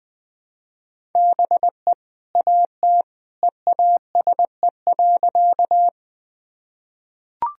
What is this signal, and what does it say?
Morse code, heard as a keyed tone, about 21 wpm, 700 Hz. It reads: BE AT EASE.
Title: Sentences from Top 1000 Words - Review Part 1 - 20wpm